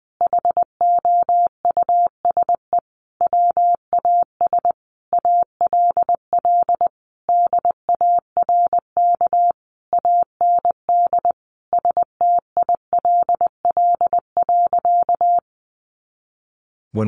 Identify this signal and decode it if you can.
Morse code, heard as a keyed tone, about 20 wpm, 700 Hz. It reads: HOUSE WAS ALL DARK AND STILL.
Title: Sentences from Top 500 Words 20wpm